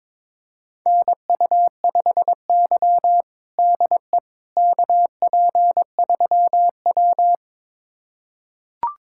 Morse code, heard as a keyed tone, about 22 wpm, 700 Hz. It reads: NU5Y DE KP3W
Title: USA Callsign DE Callsign - 4 Characters - 22wpm